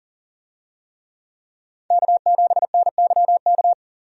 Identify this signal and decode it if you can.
Morse code, heard as a keyed tone, about 40 wpm, 700 Hz. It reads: K7NYK
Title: USA Call Signs - 5 Characters - 40wpm